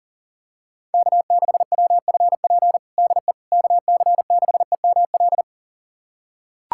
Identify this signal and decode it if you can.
Morse code, heard as a keyed tone, about 40 wpm, 700 Hz. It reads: K6WFP DE KC6EML
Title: USA Callsign DE Callsign 40wpm